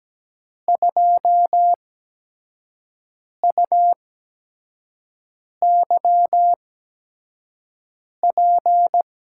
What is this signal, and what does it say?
Morse code, heard as a keyed tone, about 17 wpm, spaced as for 5 wpm, 700 Hz. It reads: 2UYP
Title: USA Call Signs - 8x Character Spacing - 17wpm